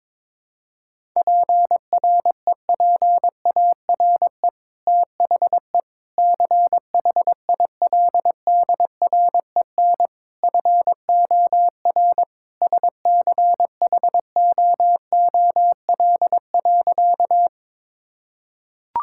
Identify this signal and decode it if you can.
Morse code, heard as a keyed tone, about 22 wpm, 700 Hz. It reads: PREPARE THE CHILDREN FOR SCHOOL.
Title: Sentences from Top 1000 Words - Review Part 6 - 22wpm